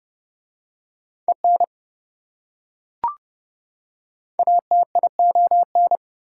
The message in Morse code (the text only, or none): EN ATION